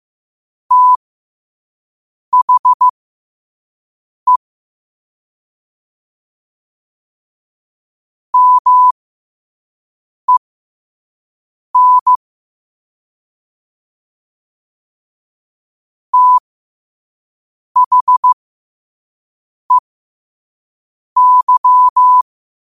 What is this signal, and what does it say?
Morse code, heard as a keyed tone, about 15 wpm, spaced as for 5 wpm, 1000 Hz. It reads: THE MEN THEY
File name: qotdfiles / 5wpm